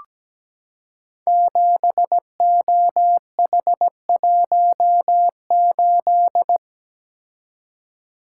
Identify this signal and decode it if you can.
Morse code, heard as a keyed tone, about 17 wpm, 700 Hz. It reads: 7OH18
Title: Code Groups - 5 Letter:Numbers - 17wpm